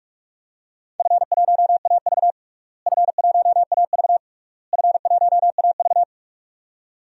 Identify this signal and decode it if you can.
Morse code, heard as a keyed tone, about 46 wpm, 700 Hz. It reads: F1AV F1AV F1AV
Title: DXCC-100 Call Signs - 4 Characters - Repeated 3x - 45wpm